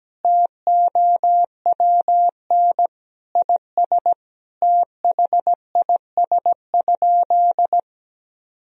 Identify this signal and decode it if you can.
Morse code, heard as a keyed tone, about 17 wpm, 700 Hz. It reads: TOWN IS THIS?
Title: Sentences from Top 1000 Words - Review Part 3 - 17wpm